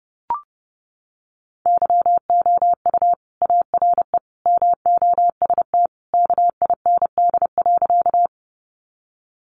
Morse code, heard as a keyed tone, about 30 wpm, 700 Hz. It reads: YOU ARE MOST KIND.
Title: Sentences from Top 200 Words 30wpm